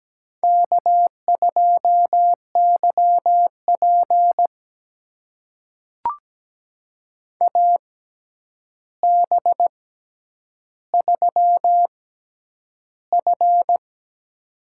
Morse code, heard as a keyed tone, about 17 wpm, spaced as for 9 wpm, 700 Hz. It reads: K2YP AB3F
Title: USA Call Signs - 4 Characters - 6x Character Spacing - 17wpm